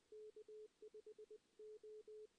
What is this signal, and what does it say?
Morse code, heard as a keyed tone, about 21 wpm, 440 Hz. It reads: K5 O
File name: 2016 JAN CW NAQP / 2016 Jan naqp k5ot - 0-3